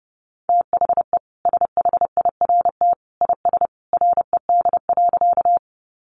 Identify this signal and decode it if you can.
Morse code, heard as a keyed tone, about 30 wpm, 700 Hz. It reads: THE SHIRT IS RED.
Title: Additional Easy Phrases 30wpm